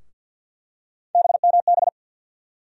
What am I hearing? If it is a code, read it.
Morse code, heard as a keyed tone, about 49 wpm, 700 Hz. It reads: BMB
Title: USA Call Sign Postfixes - 3 Letters 50wpm